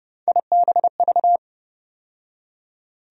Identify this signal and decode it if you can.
Morse code, heard as a keyed tone, about 32 wpm, 700 Hz. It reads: IBV
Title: USA Call Sign Postfixes - 3 Letters 30wpm